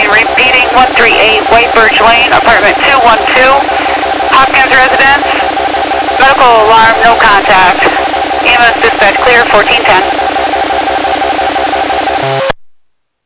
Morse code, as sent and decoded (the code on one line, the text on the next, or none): none